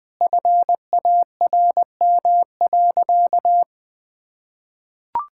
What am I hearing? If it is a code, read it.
Morse code, heard as a keyed tone, about 20 wpm, 700 Hz. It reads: FARM.
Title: Sentences from Top 1000 Words - Review Part 6 - 20wpm